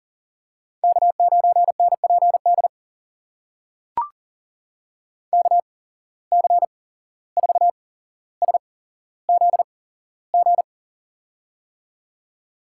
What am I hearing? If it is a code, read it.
Morse code, heard as a keyed tone, about 40 wpm, spaced as for 16 wpm, 700 Hz. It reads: K9NPD KC4SZG
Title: USA Call Signs - 8x Character Spacing - 40wpm